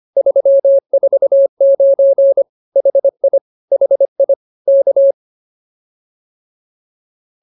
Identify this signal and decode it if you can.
Morse code, heard as a keyed tone, about 24 wpm, 550 Hz. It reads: 349 HI HI K